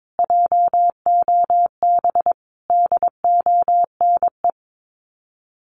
Morse code, heard as a keyed tone, about 22 wpm, 700 Hz. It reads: JOB DONE
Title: Sets of 5 Words Encore2 22wpm